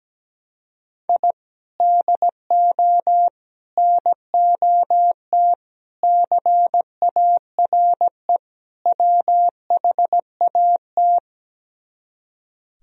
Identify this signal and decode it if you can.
Morse code, heard as a keyed tone, about 17 wpm, 700 Hz. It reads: I DO NOT CARE WHAT
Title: Speed-Racing - Sets of 5 Words 17wpm